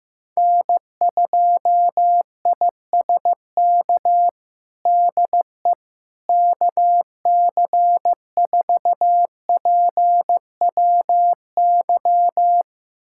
Morse code, heard as a keyed tone, about 15 wpm, 700 Hz. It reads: N2ISK DE KC4PWY
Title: USA Callsign DE Callsign 15wpm